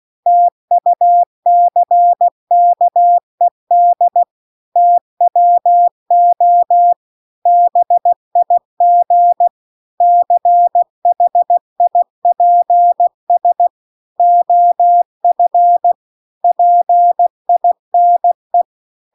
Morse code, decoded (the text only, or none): TUCKED TWO BIG CHIPS OF PINE